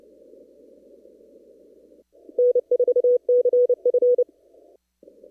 Morse code, heard as a keyed tone, about 28 wpm, 490 Hz. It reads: N4CF
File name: CW / kt5j wpx cw-op-k5pi